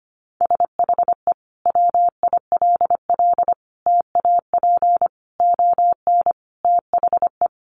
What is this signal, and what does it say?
Morse code, heard as a keyed tone, about 25 wpm, 700 Hz. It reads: SHE WILL TAP ON THE
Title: Sentences from Top 1400 Words 25wpm